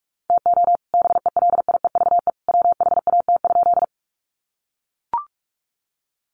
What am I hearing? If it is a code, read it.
Morse code, heard as a keyed tone, about 44 wpm, 700 Hz. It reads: TO BELIEVE WHAT?
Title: Sentences from Top 1000 Words 45wpm